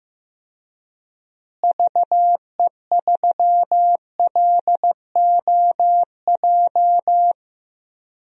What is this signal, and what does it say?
Morse code, heard as a keyed tone, about 15 wpm, 700 Hz. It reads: VE3LOJ
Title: DXCC-100 Call Signs - 6 Characters - Repeated 3x - 15wpm